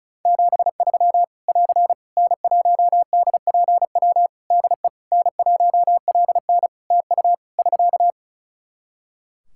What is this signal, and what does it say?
Morse code, heard as a keyed tone, about 36 wpm, 700 Hz. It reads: 73 <AR> N1DPW DE N1LN TU <SK>
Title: Practice QSO - Exchange 4 - Ending - 35wpm